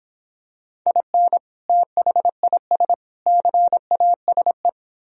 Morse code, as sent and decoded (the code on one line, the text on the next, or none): .. -. / - .... .. ... / -.-. .- ... .
IN THIS CASE